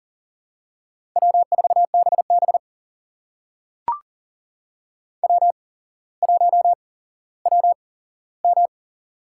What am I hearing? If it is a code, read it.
Morse code, heard as a keyed tone, about 40 wpm, spaced as for 17 wpm, 700 Hz. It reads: W4BB W1WM
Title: USA Call Signs - 8x Character Spacing - 40wpm